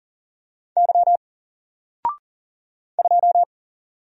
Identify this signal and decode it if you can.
Morse code, heard as a keyed tone, about 40 wpm, 700 Hz. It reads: Y 2
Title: Single Letter-Number 40wpm